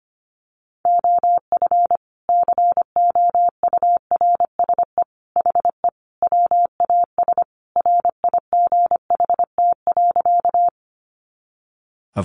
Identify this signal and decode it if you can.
Morse code, heard as a keyed tone, about 25 wpm, 700 Hz. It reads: OF COURSE HE WAS RIGHT.
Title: Sentences from Top 500 Words 25wpm